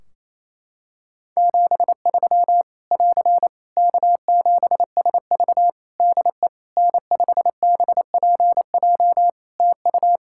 Morse code, heard as a keyed tone, about 28 wpm, 700 Hz. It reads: 73 <AR> K7SV DE N5BPJ TU
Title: Practice QSO - Exchange 4 - Ending - 28wpm